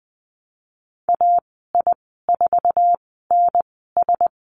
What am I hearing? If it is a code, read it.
Morse code, heard as a keyed tone, about 20 wpm, spaced as for 15 wpm, 700 Hz. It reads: AI4NS
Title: 173 Lesson 28g - Number 9 - Call Signs 20wpm